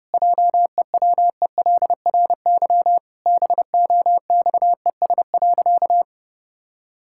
Morse code, heard as a keyed tone, about 30 wpm, 700 Hz. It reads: JEWELRY BOXES.
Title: Pangram Challenge v2 - 30wpm